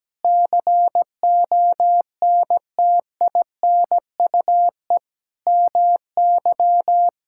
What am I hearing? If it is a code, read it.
Morse code, heard as a keyed tone, about 17 wpm, 700 Hz. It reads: CONTINUE MY
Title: Sentences from Top 1000 Words - Review Part 6 - 17wpm